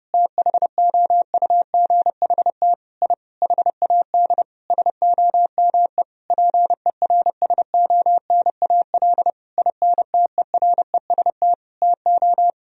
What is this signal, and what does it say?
Morse code, heard as a keyed tone, about 32 wpm, 700 Hz. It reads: THOUGHT I HAD SOME PERSONAL INTEREST TO
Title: Sentences from Top 1000 Words - Review Part 4 - 30wpm